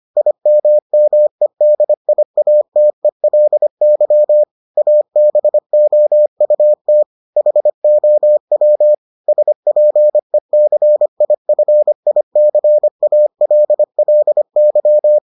Morse code, heard as a keyed tone, about 25 wpm, 600 Hz. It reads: IMMEDIATELY ABOUT HOW SPECIFICALLY